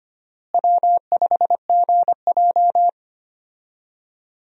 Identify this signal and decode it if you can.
Morse code, heard as a keyed tone, about 25 wpm, 700 Hz. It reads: W5GJ
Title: USA Call Signs - 4 Characters - Repeated 3x - 25wpm